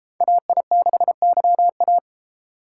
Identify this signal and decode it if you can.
Morse code, heard as a keyed tone, about 31 wpm, 700 Hz. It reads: AI6YA